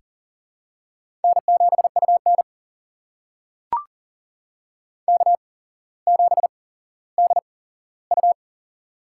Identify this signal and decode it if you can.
Morse code, heard as a keyed tone, about 40 wpm, spaced as for 17 wpm, 700 Hz. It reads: N7UN K7DU
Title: USA Call Signs - 4 Characters - 8x Character Spacing - 40wpm